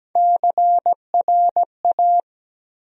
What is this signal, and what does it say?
Morse code, heard as a keyed tone, about 17 wpm, 700 Hz. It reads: CRA
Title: USA Call Sign Postfixes - 3 Letters 17wpm